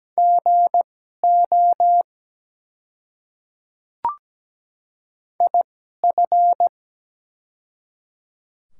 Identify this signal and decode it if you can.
Morse code, heard as a keyed tone, about 17 wpm, spaced as for 12 wpm, 690 Hz. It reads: GO IF
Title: 2x Character Spacing - Top 100 Words - 17wpm